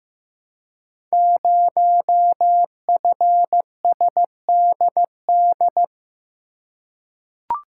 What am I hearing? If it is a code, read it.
Morse code, heard as a keyed tone, about 15 wpm, 700 Hz. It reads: 0FSDD
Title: Code Groups - 5 Letter:Numbers - 15wpm